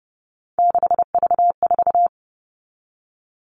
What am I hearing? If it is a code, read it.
Morse code, heard as a keyed tone, about 30 wpm, 700 Hz. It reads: 6V4
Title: All Cut Numbers - 3-Digit 30wpm